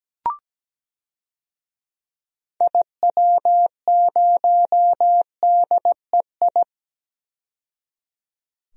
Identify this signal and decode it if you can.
Morse code, heard as a keyed tone, about 17 wpm, 700 Hz. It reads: IW0DEI
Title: DXCC-100 Call Signs - 6 Characters - 17wpm